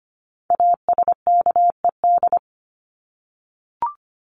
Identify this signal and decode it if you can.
Morse code, heard as a keyed tone, about 25 wpm, 700 Hz. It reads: ASKED
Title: ED Suffix - Top 100 Words with Base Words 25wpm